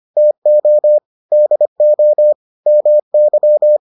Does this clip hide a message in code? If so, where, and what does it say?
Yes: Morse code from 0.2 to 3.8 s, heard as a keyed tone, about 25 wpm, 600 Hz. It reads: TO DO MY